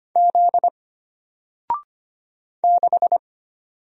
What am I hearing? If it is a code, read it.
Morse code, heard as a keyed tone, about 25 wpm, 700 Hz. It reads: Z 6
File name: Single-Letter-Number-25wpm